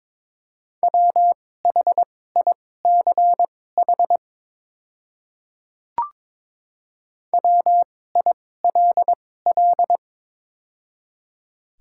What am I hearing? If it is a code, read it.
Morse code, heard as a keyed tone, about 22 wpm, spaced as for 16 wpm, 700 Hz. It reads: WHICH WILL
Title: 2x Character Spacing - Top 100 Words - 22wpm